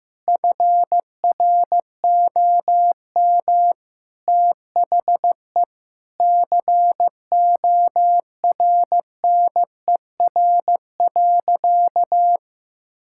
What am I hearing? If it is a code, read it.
Morse code, heard as a keyed tone, about 15 wpm, 700 Hz. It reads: FROM THE CORNER.
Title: Sentences from Top 1600 Words 15wpm